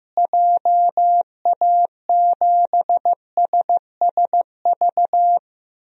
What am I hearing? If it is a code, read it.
Morse code, heard as a keyed tone, about 15 wpm, 700 Hz. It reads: JA7SSV
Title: DXCC-100 Call Signs - 6 Characters - Repeated 3x - 15wpm